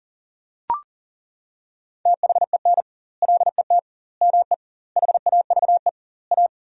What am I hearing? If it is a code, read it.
Morse code, heard as a keyed tone, about 40 wpm, 700 Hz. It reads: THEN LET ME HAVE A
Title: 2x Word-Spacing - Sentences from Top 300 Words - 40wpm